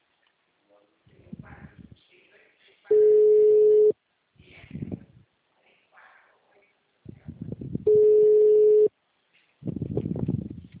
Ringback tone (Europe, 425 Hz).